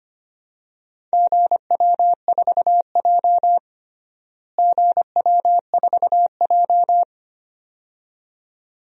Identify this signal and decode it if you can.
Morse code, heard as a keyed tone, about 25 wpm, 700 Hz. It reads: GW4J GW4J
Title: DXCC-100 Call Signs - 4 Characters - Repeated 2x - 25wpm